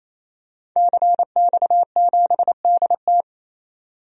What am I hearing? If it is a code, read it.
Morse code, heard as a keyed tone, about 28 wpm, 700 Hz. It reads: CX7DT
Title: DXCC-100 Call Signs - 5 Characters - 28wpm